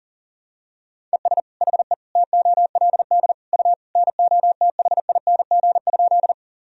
Morse code, heard as a keyed tone, about 40 wpm, 700 Hz. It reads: ES HE TOLD U NOTHING?
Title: Sentences from Top 500 Words and Abbreviations 40wpm